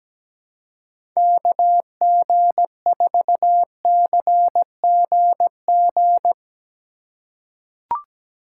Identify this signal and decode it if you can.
Morse code, heard as a keyed tone, about 17 wpm, 700 Hz. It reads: KG4CGG
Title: USA Call Signs - 3x Character Spacing - 17wpm